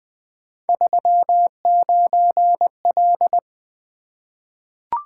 Morse code, heard as a keyed tone, about 20 wpm, 700 Hz. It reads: 39L
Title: Three Letter-Number Combinations - 20wpm